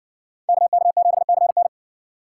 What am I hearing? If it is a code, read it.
Morse code, heard as a keyed tone, about 60 wpm, 700 Hz. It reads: KK7CN